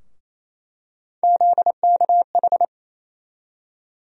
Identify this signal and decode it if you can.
Morse code, heard as a keyed tone, about 28 wpm, 700 Hz. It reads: ZKH
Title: USA Call Sign Postfixes - 3 Letters 28wpm